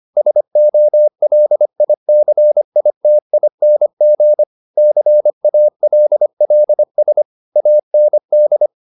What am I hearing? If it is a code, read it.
Morse code, heard as a keyed tone, about 25 wpm, 600 Hz. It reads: SOLICITING CALLS AND